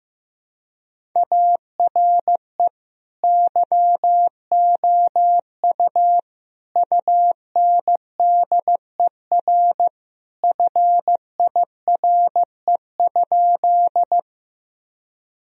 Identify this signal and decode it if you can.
Morse code, heard as a keyed tone, about 15 wpm, 700 Hz. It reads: ARE YOU UNDER FIRE?
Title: Sentences from Top 1000 Words - Review Part 1 - 15wpm